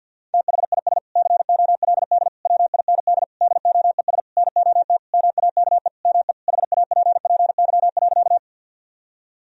Morse code, heard as a keyed tone, about 50 wpm, 700 Hz. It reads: THIS COLD WIND DOES NOT MAKE ME HAPPY.
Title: Sentences from Top 1000 Words - Review Part 2 - 50wpm